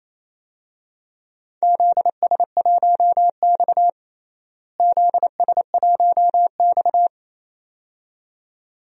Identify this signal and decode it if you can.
Morse code, heard as a keyed tone, about 28 wpm, 700 Hz. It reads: ZS1X ZS1X
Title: DXCC-100 Call Signs - 4 Characters - Repeated 2x - 28wpm